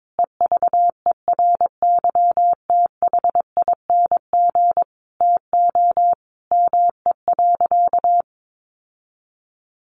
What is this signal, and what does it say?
Morse code, heard as a keyed tone, about 22 wpm, 700 Hz. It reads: EVERYTHING TO ME.
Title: Sentences from Top 700 Words 22wpm